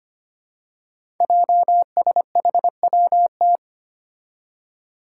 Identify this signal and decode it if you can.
Morse code, heard as a keyed tone, about 25 wpm, 700 Hz. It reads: JSHWT
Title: Five Challenging Letter-Number Combinations 25wpm